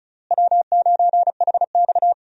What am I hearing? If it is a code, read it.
Morse code, heard as a keyed tone, about 34 wpm, 700 Hz. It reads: W9HX
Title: USA Call Signs - 4 Characters - Repeated 2x - 35wpm